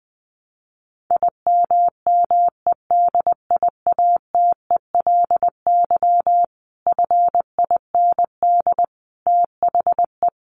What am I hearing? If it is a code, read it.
Morse code, heard as a keyed tone, about 20 wpm, 700 Hz. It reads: IMMEDIATELY FIND THE
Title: Sentences from Top 1200 Words 20wpm